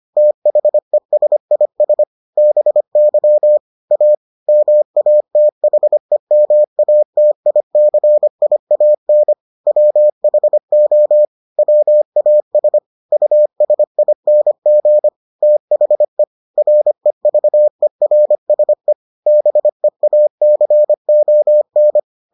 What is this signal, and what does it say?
Morse code, heard as a keyed tone, about 25 wpm, 600 Hz. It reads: THESIS BY A MATHEMATICIAN WHO WAS USING THE REVERSE BEACON